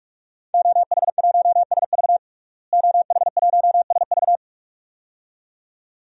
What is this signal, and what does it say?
Morse code, heard as a keyed tone, about 42 wpm, 700 Hz. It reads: OH1SV OH1SV
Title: DXCC-100 Call Signs - 5 Characters - Repeated 2x - 45wpm